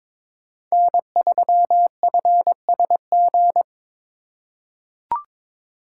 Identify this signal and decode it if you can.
Morse code, heard as a keyed tone, about 22 wpm, 700 Hz. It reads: N3FSG
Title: USA Call Signs - 5 Characters - 2x Character Spacing - 22wpm